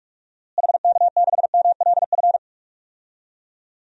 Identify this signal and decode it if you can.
Morse code, heard as a keyed tone, about 46 wpm, 700 Hz. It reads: HK6MLF
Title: DXCC-100 Call Signs - 6 Characters - Repeated 2x - 45wpm